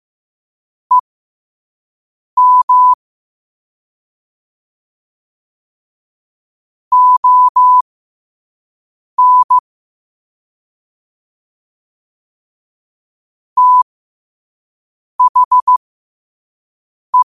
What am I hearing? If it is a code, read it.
Morse code, heard as a keyed tone, about 15 wpm, spaced as for 5 wpm, 1000 Hz. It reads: EM ON THE
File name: qotdfiles / 5wpm